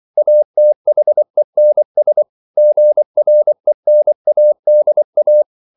Morse code, heard as a keyed tone, about 24 wpm, 600 Hz. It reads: ATHENS GRENADA
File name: CW / Practice / NationalCapitals24